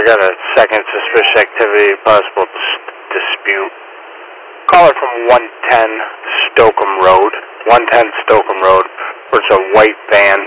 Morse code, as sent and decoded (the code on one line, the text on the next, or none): none